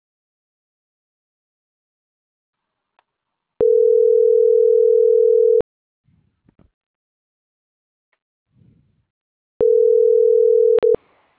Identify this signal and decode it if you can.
Ringback tone (North America), 440 + 480 Hz.